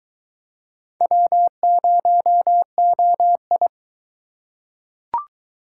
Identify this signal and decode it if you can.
Morse code, heard as a keyed tone, about 23 wpm, 700 Hz. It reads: W0OI